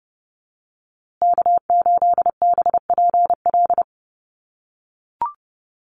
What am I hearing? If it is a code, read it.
Morse code, heard as a keyed tone, about 30 wpm, 700 Hz. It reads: K8BPL
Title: USA Call Signs Repeated 2x 30wpm